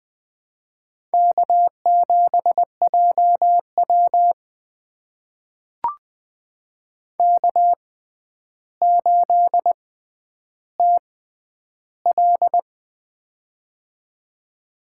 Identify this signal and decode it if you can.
Morse code, heard as a keyed tone, about 20 wpm, spaced as for 10 wpm, 700 Hz. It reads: K7JW K8TL
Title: USA Call Signs - 4 Characters - 6x Character Spacing - 20wpm